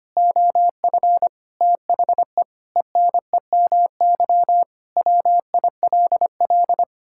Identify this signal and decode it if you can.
Morse code, heard as a keyed tone, about 25 wpm, 700 Hz. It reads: OF THE ENEMY WILL
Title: Pangram Challenge v2 - 25wpm